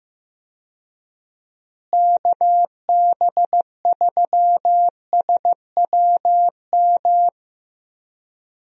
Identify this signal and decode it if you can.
Morse code, heard as a keyed tone, about 15 wpm, 700 Hz. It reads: KB3SWM
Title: USA Call Signs - 6 Characters - Repeated 3x - 15wpm